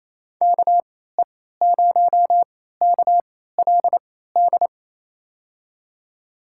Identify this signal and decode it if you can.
Morse code, heard as a keyed tone, about 27 wpm, spaced as for 16 wpm, 700 Hz. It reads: KE0KLD
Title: USA Call Signs - 6 Characters - 3x Character Spacing - 28wpm